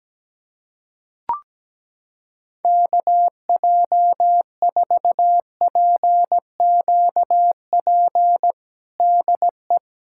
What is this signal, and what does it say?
Morse code, heard as a keyed tone, about 17 wpm, 700 Hz. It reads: KJ4PQP DE
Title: USA Callsign DE Callsign - 6 Characters - 17wpm